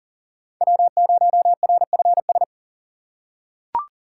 Morse code, heard as a keyed tone, about 40 wpm, 700 Hz. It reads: W0RFS
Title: USA Call Signs - 5 Characters - 3x Character Spacing - 40wpm